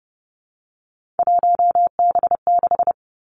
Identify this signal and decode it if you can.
Morse code, heard as a keyed tone, about 30 wpm, 700 Hz. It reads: 1B6
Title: Three Challenging Letter-Number Combinations 30wpm